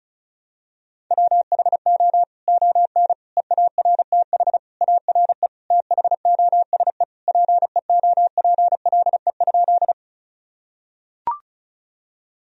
Morse code, heard as a keyed tone, about 36 wpm, 700 Hz. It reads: WHO ON EARTH ARE THOSE PEOPLE?
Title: Sentences from Top 1000 Words - Review Part 1 - 35wpm